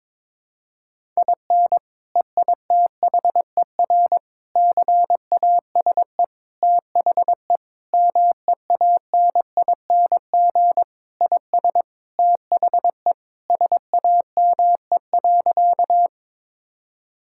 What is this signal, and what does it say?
Morse code, heard as a keyed tone, about 22 wpm, 700 Hz. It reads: IN EITHER CASE THE MEANING IS THE SAME.
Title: Sentences from Top 1000 Words - Review Part 6 - 22wpm